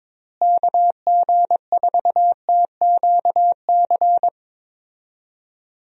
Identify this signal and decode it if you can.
Morse code, heard as a keyed tone, about 21 wpm, 700 Hz. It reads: KG4TQC
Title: USA Call Signs - 6 Characters - Repeated 2x - 22wpm